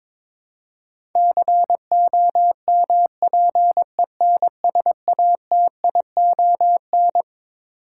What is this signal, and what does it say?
Morse code, heard as a keyed tone, about 22 wpm, 700 Hz. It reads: COMPENSATION
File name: Twelve Letter Words - 22wpm